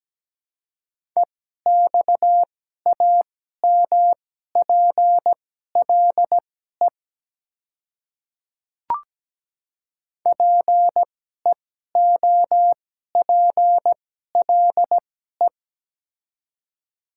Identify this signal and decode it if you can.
Morse code, heard as a keyed tone, about 17 wpm, spaced as for 12 wpm, 700 Hz. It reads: EXAMPLE PEOPLE
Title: 2x Character Spacing - Top 300 Words - 17wpm